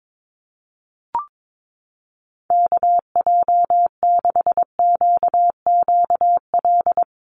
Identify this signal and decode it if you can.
Morse code, heard as a keyed tone, about 22 wpm, 700 Hz. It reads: KJ6QQL
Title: USA Call Signs Repeated 3x 22wpm